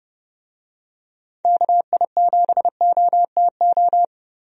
Morse code, heard as a keyed tone, about 30 wpm, 700 Hz. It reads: KI7OTO
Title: USA Call Signs - 6 Characters - 30wpm